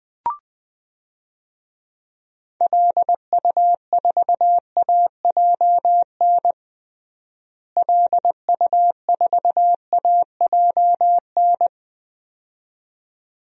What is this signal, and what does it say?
Morse code, heard as a keyed tone, about 20 wpm, 700 Hz. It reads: LU4AJN LU4AJN
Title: DXCC-100 Call Signs - 6 Characters - Repeated 2x - 20wpm